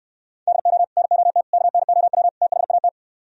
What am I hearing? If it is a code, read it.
Morse code, heard as a keyed tone, about 68 wpm, 700 Hz. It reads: DO NOT KNOW THAT